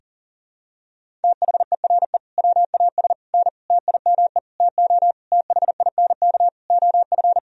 Morse code, heard as a keyed tone, about 37 wpm, 700 Hz. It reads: THERE WAS N TIME TO THINK OF
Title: Sentences from Top 400 Words and Abbreviations 40wpm